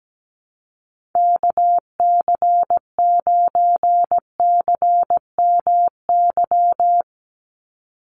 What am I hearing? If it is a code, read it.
Morse code, heard as a keyed tone, about 17 wpm, 700 Hz. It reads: KC9CMY
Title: USA Call Signs 17wpm